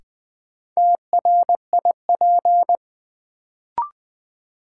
Morse code, heard as a keyed tone, about 20 wpm, 700 Hz. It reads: TRIP
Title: Sentences from Top 1000 Words - Review Part 4 - New Word Drill - 20wpm